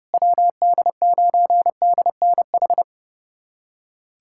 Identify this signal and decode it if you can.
Morse code, heard as a keyed tone, about 30 wpm, 700 Hz. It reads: WD9DNH
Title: USA Callsign DE Callsign - 6 Characters - 30wpm